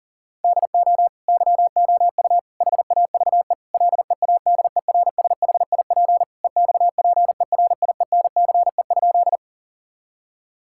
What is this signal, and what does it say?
Morse code, heard as a keyed tone, about 40 wpm, 700 Hz. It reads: DO YOU HAVE LEADERSHIP EXPERIENCE?
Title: Sentences from Top 1600 Words 40wpm